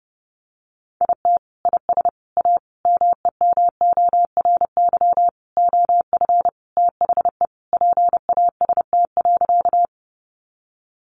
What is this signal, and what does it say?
Morse code, heard as a keyed tone, about 30 wpm, 700 Hz. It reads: IT IS A MEMORY OF THE PAST.
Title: Sentences from Top 1300 Words 30wpm